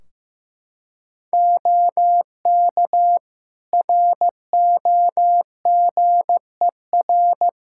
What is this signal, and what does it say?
Morse code, heard as a keyed tone, about 15 wpm, 700 Hz. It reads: OK ROGER
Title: Practice QSO - Exchange 4 - Ending - 15wpm